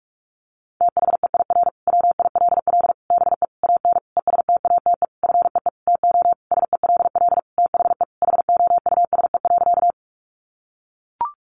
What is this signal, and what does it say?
Morse code, heard as a keyed tone, about 44 wpm, 700 Hz. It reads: THEIR WILL BE AN ESTATE FEE TO SELL THE HOUSE.
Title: Sentences from Top 1500 Words 45wpm